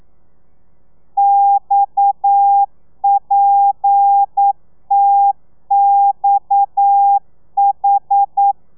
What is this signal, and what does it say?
Morse code, heard as a keyed tone, about 9 wpm, 800 Hz. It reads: XPTXH
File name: telegrafia / i2tfj / Lez7 40